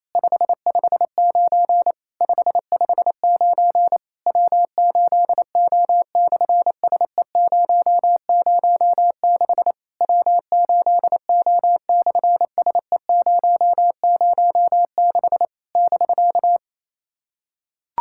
Morse code, heard as a keyed tone, about 27 wpm, 700 Hz. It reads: 559 559 W8O/SE006 W8O/SE006 <BK>
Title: SOTA QSO Practice 28wpm